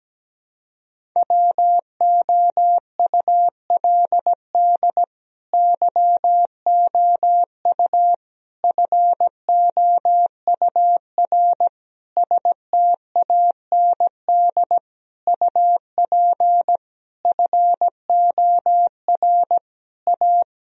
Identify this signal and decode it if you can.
Morse code, heard as a keyed tone, about 17 wpm, 700 Hz. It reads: WOULD YOU FOUR STAND UP FOR A
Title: Sentences from Top 1000 Words - Review Part 1 - 17wpm